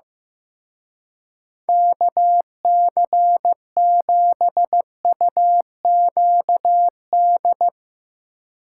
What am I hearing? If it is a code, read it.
Morse code, heard as a keyed tone, about 15 wpm, 700 Hz. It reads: KC7UQD
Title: Speed Racing - USA 2x3 Call Signs 15wpm